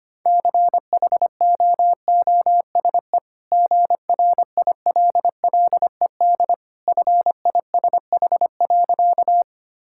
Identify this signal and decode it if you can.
Morse code, heard as a keyed tone, about 25 wpm, 690 Hz. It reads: CHOOSE GRILLED FISH.
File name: Easy Sentences with Hints - 25wpm